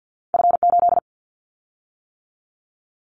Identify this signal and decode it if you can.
Morse code, heard as a keyed tone, about 50 wpm, 700 Hz. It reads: F8
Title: Callsign Prefixes for Top 50 DXCC - 50wpm